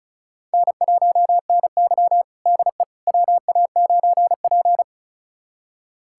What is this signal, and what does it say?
Morse code, heard as a keyed tone, about 36 wpm, 700 Hz. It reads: N1NY DE WA9P
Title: USA Callsign DE Callsign - 4 Characters - 35wpm